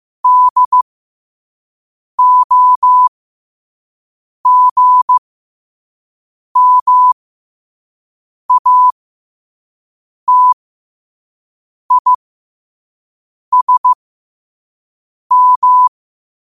Morse code, heard as a keyed tone, about 15 wpm, spaced as for 5 wpm, 1000 Hz. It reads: DOGMATISM
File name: qotdfiles / 5wpm